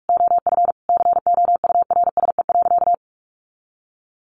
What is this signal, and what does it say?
Morse code, heard as a keyed tone, about 44 wpm, 700 Hz. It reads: OF COURSE.
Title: Sentences from Top 900 Words 45wpm